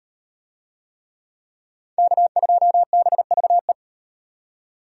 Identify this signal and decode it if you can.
Morse code, heard as a keyed tone, about 37 wpm, 700 Hz. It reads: K2BVE